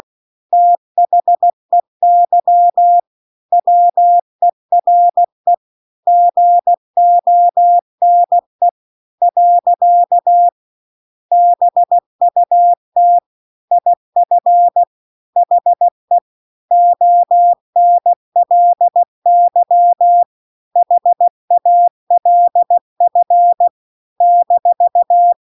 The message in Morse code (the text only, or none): THEY WERE GONE . BUT IF HE ONLY HALF -